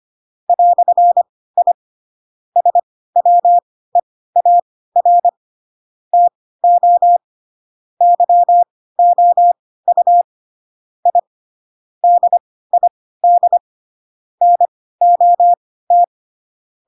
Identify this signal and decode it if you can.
Morse code, heard as a keyed tone, about 25 wpm, spaced as for 16 wpm, 700 Hz. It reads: "I SWEAR TO YOU I DID NOT
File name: Pinocchio / Pinocchio ch2 10 25 ik1hge